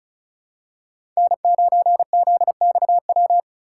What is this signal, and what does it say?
Morse code, heard as a keyed tone, about 36 wpm, 700 Hz. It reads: N9ZXW
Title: USA Call Signs - 3x Character Spacing - 35wpm